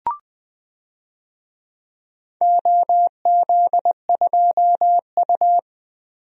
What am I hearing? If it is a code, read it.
Morse code, heard as a keyed tone, about 20 wpm, 700 Hz. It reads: OZ2U
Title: DXCC-100 Call Signs - 4 Characters - 20wpm